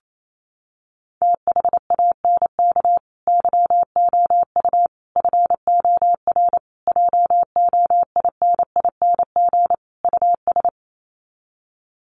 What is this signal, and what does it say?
Morse code, heard as a keyed tone, about 28 wpm, 700 Hz. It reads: THANK YOU FOR JOINING US
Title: Sets of 5 Words 28wpm